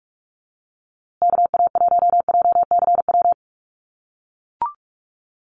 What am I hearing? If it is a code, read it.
Morse code, heard as a keyed tone, about 44 wpm, 690 Hz. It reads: KA1JCW
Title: USA Call Signs Repeated 2x 45wpm